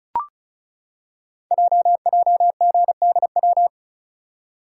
Morse code, heard as a keyed tone, about 36 wpm, 700 Hz. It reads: JJGDW
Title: Code Groups - 5 Letters - 35wpm